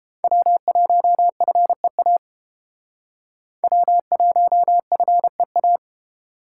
Morse code, heard as a keyed tone, about 30 wpm, 700 Hz. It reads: W1FEA W1FEA